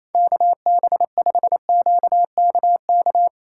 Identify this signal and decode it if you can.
Morse code, heard as a keyed tone, about 28 wpm, 700 Hz. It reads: KB5QKK
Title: USA Callsign DE Callsign - 6 Characters - 28wpm